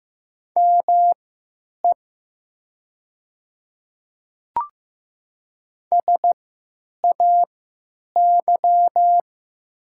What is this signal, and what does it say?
Morse code, heard as a keyed tone, about 15 wpm, spaced as for 9 wpm, 700 Hz. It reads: ME SAY
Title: 3x Character Spacing - Top 300 Words - 15wpm